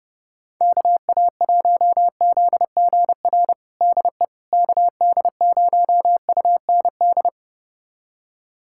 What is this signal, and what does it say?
Morse code, heard as a keyed tone, about 30 wpm, 700 Hz. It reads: KA1ZGR DE KD0UND
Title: USA Callsign DE Callsign - 6 Characters - 30wpm